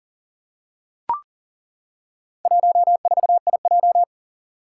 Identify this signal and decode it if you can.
Morse code, heard as a keyed tone, about 40 wpm, 700 Hz. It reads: 14IJ
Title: Four Challenging Letter-Number Combinations 40wpm